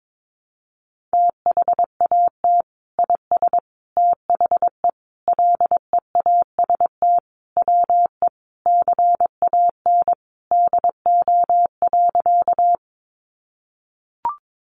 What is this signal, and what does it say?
Morse code, heard as a keyed tone, about 22 wpm, 700 Hz. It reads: THAT IS THE LEAST WE CAN DO.
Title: Sentences from Top 700 Words 22wpm